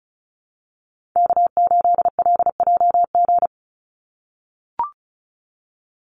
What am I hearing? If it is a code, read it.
Morse code, heard as a keyed tone, about 36 wpm, 700 Hz. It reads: K8LJG
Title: NA SOTA Chaser Callsigns 35wpm